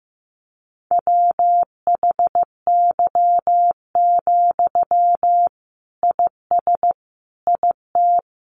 Morse code, heard as a keyed tone, about 15 wpm, 700 Hz. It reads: WHY, IS IT
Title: Sentences from Top 300 Words 15wpm